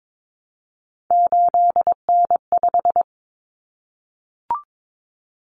Morse code, heard as a keyed tone, about 22 wpm, 700 Hz. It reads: 8N5
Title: Common Cut Numbers - 3-Digit 22wpm